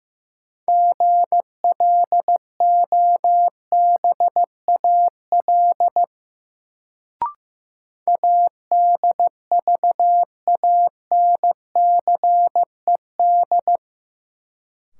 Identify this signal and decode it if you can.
Morse code, heard as a keyed tone, about 15 wpm, 700 Hz. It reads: GLOBAL ADVANCED
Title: Sentences from Top 1200 Words - New Word Drill - 15wpm